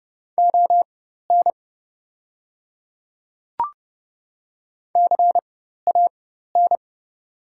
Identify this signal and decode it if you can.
Morse code, heard as a keyed tone, about 30 wpm, spaced as for 14 wpm, 700 Hz. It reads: ON CAN
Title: 4x Character Spacing - Top 100 Words - 30wpm